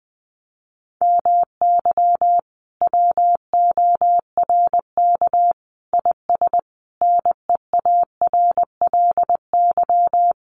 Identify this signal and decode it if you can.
Morse code, heard as a keyed tone, about 20 wpm, 700 Hz. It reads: MY WORK IS NEARLY
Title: Sentences from Top 1300 Words 20wpm